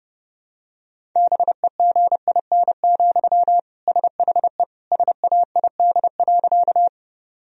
Morse code, heard as a keyed tone, about 30 wpm, 700 Hz. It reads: BEGIN, SHE SAID.
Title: Sentences from Top 1000 Words - Review Part 1 - 30wpm